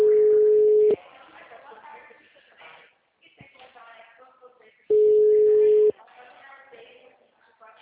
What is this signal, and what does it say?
Ringback tone (Europe, 425 Hz), with speech.
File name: out- -2004- - - .66